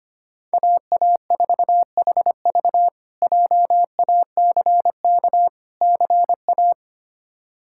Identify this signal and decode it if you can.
Morse code, heard as a keyed tone, about 25 wpm, 700 Hz. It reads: AA4HV JACK CA
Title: K1USN SST QSO Practice - 25wpm